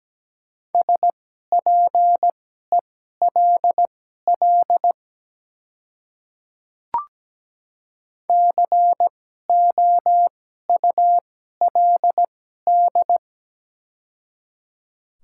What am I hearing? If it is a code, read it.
Morse code, heard as a keyed tone, about 17 wpm, spaced as for 12 wpm, 700 Hz. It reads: SPELL COULD
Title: 2x Character Spacing - Top 200 Words - 17wpm